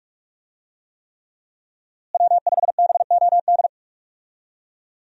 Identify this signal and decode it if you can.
Morse code, heard as a keyed tone, about 47 wpm, 700 Hz. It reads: W5BOD
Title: USA Call Signs - 5 Characters - Repeated 3x - 45wpm